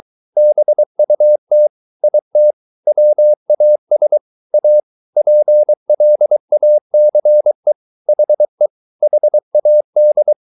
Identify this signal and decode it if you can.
Morse code, heard as a keyed tone, about 23 wpm, 600 Hz. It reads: BUT IT WAS A PLACE HE HAD